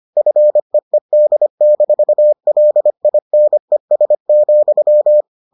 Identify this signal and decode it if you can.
Morse code, heard as a keyed tone, about 25 wpm, 600 Hz. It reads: FEED-LINES,